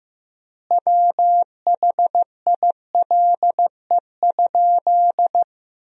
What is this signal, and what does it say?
Morse code, heard as a keyed tone, about 15 wpm, 700 Hz. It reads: WHILE?
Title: 3x Word-Spacing - Sentences from Top 300 Words - 15wpm